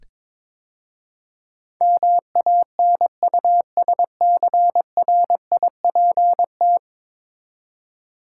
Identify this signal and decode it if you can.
Morse code, heard as a keyed tone, about 22 wpm, 700 Hz. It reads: MANUSCRIPT
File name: practice / 22wpm / 10 Letter Words - First 3 Letters Given - 22wpm